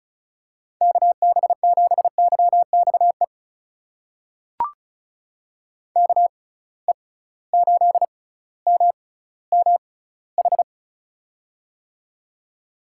Morse code, heard as a keyed tone, about 34 wpm, spaced as for 18 wpm, 700 Hz. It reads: KB7YXE KE8MMH
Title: USA Call Signs - 6 Characters - 6x Character Spacing - 35wpm